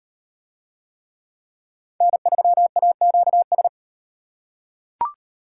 Morse code, heard as a keyed tone, about 33 wpm, 700 Hz. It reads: N3AQS